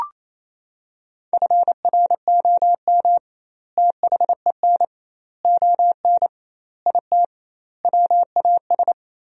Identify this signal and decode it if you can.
Morse code, heard as a keyed tone, about 28 wpm, 700 Hz. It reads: FROM THEN ON IT WAS
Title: 2x Word-Spacing - Sentences from Top 400 Words - 28wpm